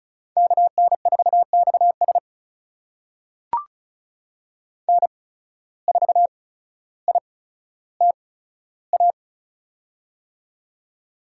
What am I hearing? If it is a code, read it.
Morse code, heard as a keyed tone, about 36 wpm, spaced as for 15 wpm, 700 Hz. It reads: KN4XS N4ITA
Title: USA Call Signs - 5 Characters - 8x Character Spacing - 35wpm